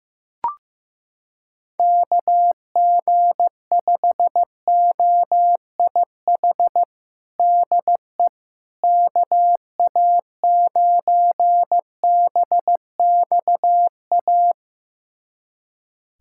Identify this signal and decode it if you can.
Morse code, heard as a keyed tone, about 16 wpm, 700 Hz. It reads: KG5OIH DE KA9BXA
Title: USA Callsign DE Callsign - 6 Characters - 15wpm